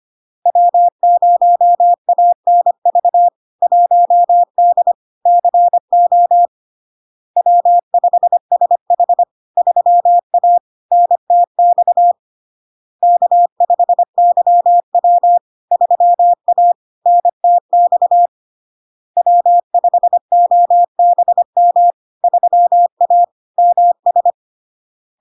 Morse code, heard as a keyed tone, about 25 wpm, 700 Hz. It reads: W0ANV 1D CO W5SH 3A NTX K5YW 3A NTX W5OBM 3A MS
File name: LARK-FD-2022-Log-in-CW-25wpm-700-Hz